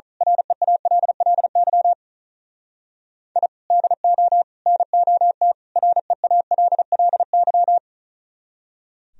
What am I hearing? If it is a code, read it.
Morse code, heard as a keyed tone, about 37 wpm, 700 Hz. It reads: REALLY I DO NOT REALLY
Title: Speed-Racing - Sets of 4 Words 35wpm